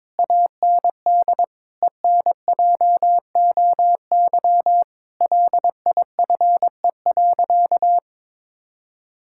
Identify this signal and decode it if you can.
Morse code, heard as a keyed tone, about 22 wpm, 700 Hz. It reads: AND ENJOY LIFE.
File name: practice / 22wpm / Sentences with 8-Word Phrases - 22wpm